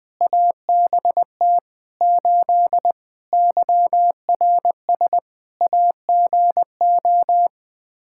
Morse code, heard as a keyed tone, about 21 wpm, 700 Hz. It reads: ABT 8 YRS AGO
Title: Practice QSO - Exchange 3 - More Info - 20wpm